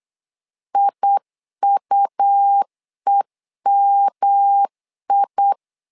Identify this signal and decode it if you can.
Morse code, heard as a keyed tone, about 8 wpm, 800 Hz. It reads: IUEMI
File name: LEZIONI / 3 MWRO / 3 MWR 8wpm 2444